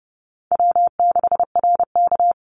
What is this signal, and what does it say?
Morse code, heard as a keyed tone, about 30 wpm, 700 Hz. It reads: W6RK
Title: Call Signs Repeated 3x - 30wpm